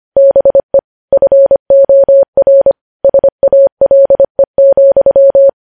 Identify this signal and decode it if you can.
Morse code, heard as a keyed tone, about 25 wpm, 550 Hz. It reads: BE FOR SALE,
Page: news headlines in morse code at 25 wpm